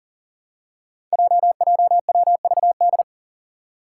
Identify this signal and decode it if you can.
Morse code, heard as a keyed tone, about 40 wpm, 700 Hz. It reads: JJWVD
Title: Code Groups - 5 Letters - 40wpm